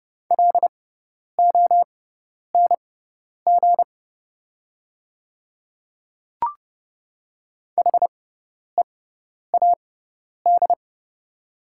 Morse code, heard as a keyed tone, about 31 wpm, spaced as for 10 wpm, 700 Hz. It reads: LONG HEAD
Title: 6x Character Spacing - Top 500 Words - 30wpm